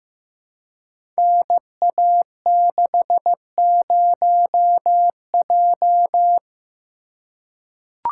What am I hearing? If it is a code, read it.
Morse code, heard as a keyed tone, about 15 wpm, 700 Hz. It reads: NA60J
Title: Code Groups - 5 Letter:Numbers - 15wpm